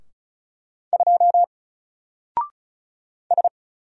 Morse code, heard as a keyed tone, about 36 wpm, 700 Hz. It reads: 2 S
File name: Single-Letter-Number-35wpm